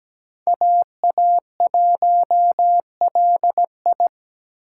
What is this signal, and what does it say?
Morse code, heard as a keyed tone, about 20 wpm, 700 Hz. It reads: AA1LI
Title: Speed Racing - USA 2x2 Call Signs 17wpm